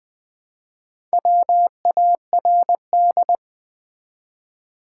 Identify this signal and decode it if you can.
Morse code, heard as a keyed tone, about 20 wpm, 700 Hz. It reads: WARD
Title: Build A Word - 1 Letter At A Time - 4-Letter Words - 20wpm